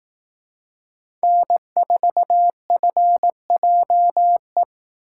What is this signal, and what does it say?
Morse code, heard as a keyed tone, about 18 wpm, 700 Hz. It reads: N4FJE
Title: Speed Racing - USA 1x3 Call Signs 15wpm